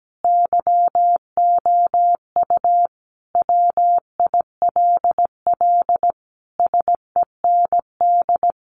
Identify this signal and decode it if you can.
Morse code, heard as a keyed tone, about 17 wpm, 700 Hz. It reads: YOU WILL SEND
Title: Sentences from Top 1500 Words 17wpm